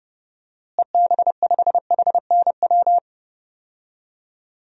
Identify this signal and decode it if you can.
Morse code, heard as a keyed tone, about 30 wpm, 700 Hz. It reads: EB5HNW
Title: DXCC-100 Call Signs - 6 Characters - Repeated 2x - 30wpm